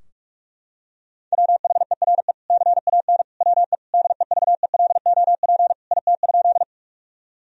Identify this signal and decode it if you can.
Morse code, heard as a keyed tone, about 41 wpm, 700 Hz. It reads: WHERE CAN WE DEVELOP IT?
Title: Sentences from Top 1000 Words - Review Part 2 - 45wpm